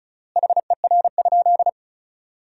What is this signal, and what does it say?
Morse code, heard as a keyed tone, about 36 wpm, 700 Hz. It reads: HER?
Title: 3x Word-Spacing - Sentences from Top 100 Words - 35wpm